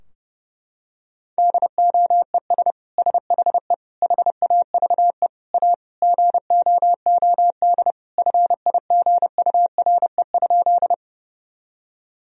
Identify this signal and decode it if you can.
Morse code, heard as a keyed tone, about 32 wpm, 700 Hz. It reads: DOES SHE HAVE A GOOD FIGURE?
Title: Sentences from Top 1000 Words - Review Part 2 - 30wpm